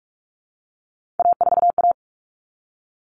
Morse code, heard as a keyed tone, about 44 wpm, 700 Hz. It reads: A4A
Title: All Cut Numbers - 3-Digit 45wpm